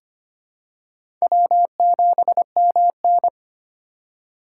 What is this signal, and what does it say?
Morse code, heard as a keyed tone, about 25 wpm, 700 Hz. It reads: W7MN
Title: USA Call Signs - 4 Characters - 3x Character Spacing - 25wpm